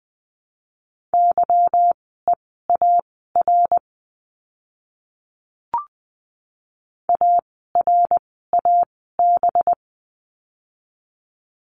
Morse code, heard as a keyed tone, about 20 wpm, spaced as for 14 wpm, 700 Hz. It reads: YEAR ARAB
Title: 185 Lesson 30f - Letter b - Words 20wpm